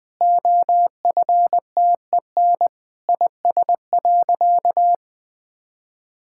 Morse code, heard as a keyed tone, about 21 wpm, 700 Hz. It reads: OFTEN IS.
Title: Sentences with 9-Word Phrases - Spoken Beginning - 20wpm